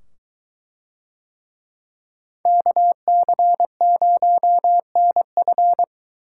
Morse code, heard as a keyed tone, about 21 wpm, 700 Hz. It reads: KC0NF